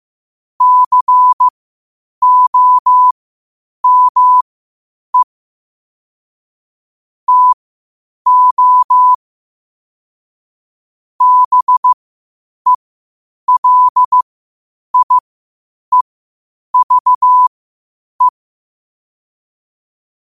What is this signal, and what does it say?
Morse code, heard as a keyed tone, about 15 wpm, spaced as for 8 wpm, 1000 Hz. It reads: COME TO BELIEVE
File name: qotdfiles / 8wpm